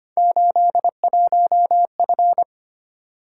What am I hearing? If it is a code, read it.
Morse code, heard as a keyed tone, about 29 wpm, 700 Hz. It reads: 81F
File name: Speed-Racing - Three Letter-Number Combinations - 25wpm